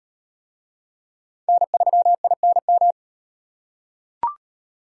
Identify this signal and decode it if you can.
Morse code, heard as a keyed tone, about 38 wpm, 700 Hz. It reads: N3INM